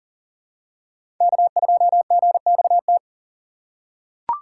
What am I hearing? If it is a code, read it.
Morse code, heard as a keyed tone, about 40 wpm, 700 Hz. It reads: K2GXT
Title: USA Call Signs - 4x Character Spacing - 40wpm